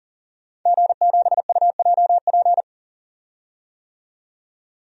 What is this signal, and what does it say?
Morse code, heard as a keyed tone, about 40 wpm, 700 Hz. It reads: G7UJP
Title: DXCC-100 Call Signs - 5 Characters - 40wpm